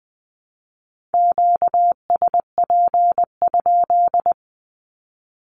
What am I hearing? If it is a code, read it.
Morse code, heard as a keyed tone, about 20 wpm, 700 Hz. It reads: QSP?
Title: Q-Codes 20wpm